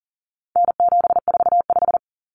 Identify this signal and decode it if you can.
Morse code, heard as a keyed tone, about 40 wpm, 700 Hz. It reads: N745
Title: Common Cut Numbers - 4-Digit 40wpm